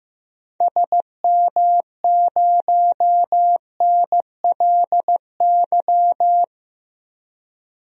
Morse code, heard as a keyed tone, about 15 wpm, 700 Hz. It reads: SM0NLY
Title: DXCC-100 Call Signs - 6 Characters - 15wpm